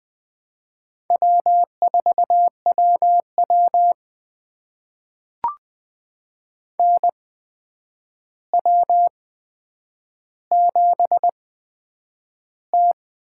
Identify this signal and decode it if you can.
Morse code, heard as a keyed tone, about 20 wpm, spaced as for 9 wpm, 700 Hz. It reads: W4WW NW7T
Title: USA Call Signs - 4 Characters - 8x Character Spacing - 20wpm